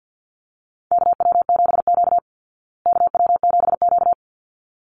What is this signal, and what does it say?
Morse code, heard as a keyed tone, about 50 wpm, 700 Hz. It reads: KW7Q KW7Q
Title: Call Signs Repeated 3x - 50wpm